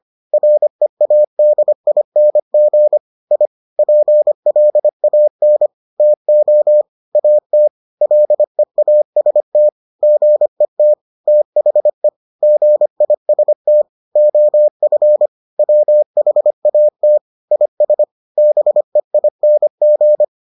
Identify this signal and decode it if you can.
Morse code, heard as a keyed tone, about 25 wpm, 600 Hz. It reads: READING I PLAN TO AT LEAST GET THE GIST OF WHAT IS BEING